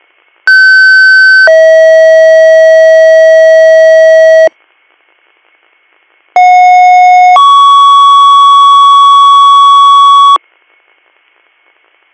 Two-tone page, heard as a one-second tone, then a three-second tone, about 1550 then 640 Hz, about 730 then 1120 Hz. It is heard from 0.5 to 10.4 s.